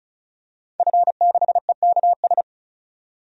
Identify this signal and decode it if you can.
Morse code, heard as a keyed tone, about 34 wpm, 700 Hz. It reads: F6EKS
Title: DXCC-100 Call Signs - 5 Characters - 35wpm